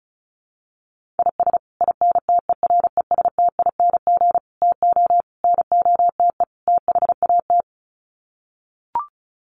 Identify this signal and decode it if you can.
Morse code, heard as a keyed tone, about 34 wpm, 700 Hz. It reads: IS INTERESTING TO NOTE THAT
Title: Sets of 5 Words Encore2 35wpm